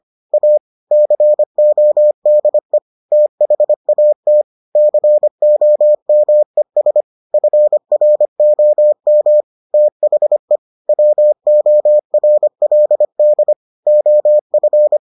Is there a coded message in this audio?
Yes: Morse code from 0.3 to 15.0 s, heard as a keyed tone, about 25 wpm, 600 Hz. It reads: A CODE THAT COMES FROM THE WORLD OF